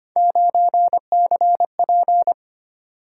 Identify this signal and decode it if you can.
Morse code, heard as a keyed tone, about 29 wpm, 700 Hz. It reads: 9CP
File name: Speed-Racing - Three Letter-Number Combinations - 25wpm